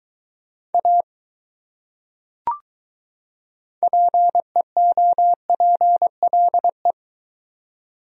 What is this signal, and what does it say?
Morse code, heard as a keyed tone, about 23 wpm, 700 Hz. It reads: A PEOPLE